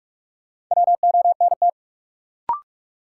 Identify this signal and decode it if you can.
Morse code, heard as a keyed tone, about 47 wpm, 700 Hz. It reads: WONT
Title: Sentences from Top 1000 Words - Review Part 6 - New Word Drill - 45wpm